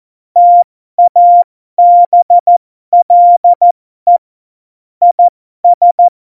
Morse code, heard as a keyed tone, about 14 wpm, spaced as for 12 wpm, 700 Hz. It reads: TABLE IS
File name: Morsekurs / Witze